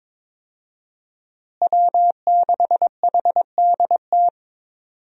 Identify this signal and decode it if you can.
Morse code, heard as a keyed tone, about 22 wpm, 700 Hz. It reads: W6HDT
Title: USA Call Signs - 5 Characters - Repeated 3x - 22wpm